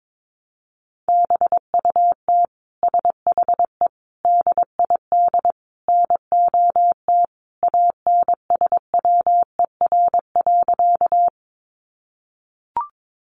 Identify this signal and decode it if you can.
Morse code, heard as a keyed tone, about 22 wpm, 700 Hz. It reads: BUT SHE DID NOT ANSWER.
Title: Sentences from Top 600 Words 22wpm